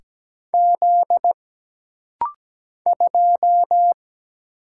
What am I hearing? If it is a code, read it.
Morse code, heard as a keyed tone, about 17 wpm, 700 Hz. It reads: Z 2
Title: Single Letter-Number - Rapid-Fire 17wpm